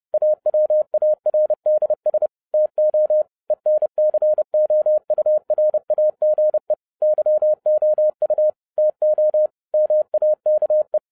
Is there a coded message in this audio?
Yes: Morse code from 0.1 to 11.0 s, heard as a keyed tone, about 30 wpm, 610 Hz. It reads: AWARDS TO ENCOURAGE YOU TO MAKE